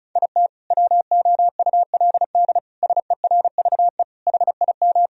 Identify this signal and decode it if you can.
Morse code, heard as a keyed tone, about 34 wpm, 700 Hz. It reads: IT WOULD SERVE HIM
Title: Sentences from Top 1000 Words - Review Part 2 - 35wpm